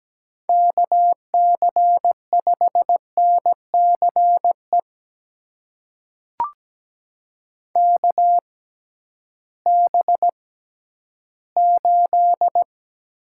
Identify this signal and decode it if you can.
Morse code, heard as a keyed tone, about 17 wpm, spaced as for 11 wpm, 700 Hz. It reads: KC5NCE KB8
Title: USA Call Signs - 6 Characters - 6x Character Spacing - 17wpm